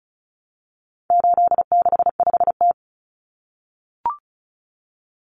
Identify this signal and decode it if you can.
Morse code, heard as a keyed tone, about 34 wpm, 700 Hz. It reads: 865T
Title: Common Cut Numbers - 4-Digit 35wpm